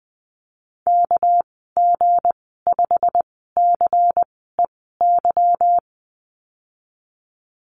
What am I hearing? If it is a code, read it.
Morse code, heard as a keyed tone, about 20 wpm, spaced as for 14 wpm, 700 Hz. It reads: KG5CEY
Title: 213 Lesson 34g - Number 8 - Call Signs 20wpm